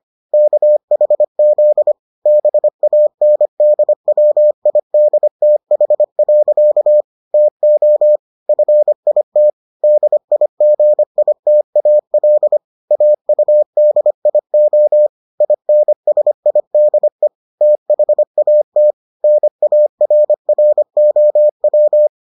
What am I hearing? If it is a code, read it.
Morse code, heard as a keyed tone, about 25 wpm, 600 Hz. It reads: KHZ BANDWIDTH. TO FIT DIGITAL AUDIO INSIDE THAT NARROW